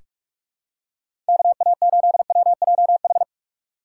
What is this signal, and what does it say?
Morse code, heard as a keyed tone, about 45 wpm, 690 Hz. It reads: KA8WJH
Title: USA Call Signs - 4x Character Spacing - 45wpm